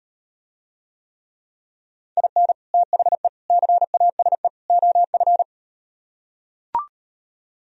Morse code, heard as a keyed tone, about 39 wpm, 700 Hz. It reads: IN THE CASE OF